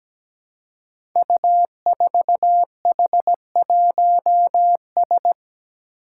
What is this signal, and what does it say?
Morse code, heard as a keyed tone, about 17 wpm, 700 Hz. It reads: U4H1S
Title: Five Challenging Letter-Number Combinations 17wpm